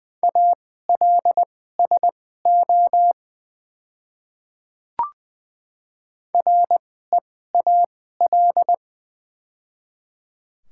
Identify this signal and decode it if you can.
Morse code, heard as a keyed tone, about 20 wpm, spaced as for 14 wpm, 700 Hz. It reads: ALSO REAL
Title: 2x Character Spacing - Top 300 Words - 20wpm